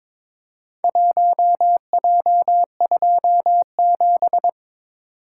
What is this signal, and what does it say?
Morse code, heard as a keyed tone, about 22 wpm, 700 Hz. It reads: 1J27
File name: practice / 22wpm / Four Letter-Number Combinations - 22wpm